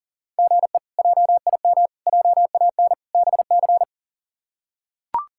Morse code, heard as a keyed tone, about 40 wpm, 700 Hz. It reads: GE JIM JAN BC
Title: K1USN SST QSO Practice - 40wpm